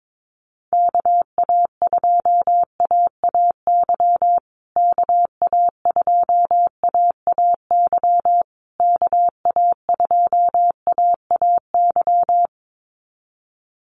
Morse code, heard as a keyed tone, about 22 wpm, 700 Hz. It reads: KA2AAY KA2AAY KA2AAY
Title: USA Call Signs Repeated 3x 22wpm